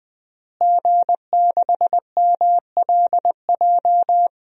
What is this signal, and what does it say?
Morse code, heard as a keyed tone, about 20 wpm, 700 Hz. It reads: G6MLJ
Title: DXCC-100 Call Signs - 5 Characters - Repeated 2x - 20wpm